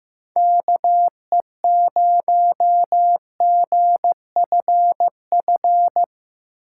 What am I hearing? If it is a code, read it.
Morse code, heard as a keyed tone, about 15 wpm, 700 Hz. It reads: KE0GFF
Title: USA Call Signs - 6 Characters - 4x Character Spacing - 15wpm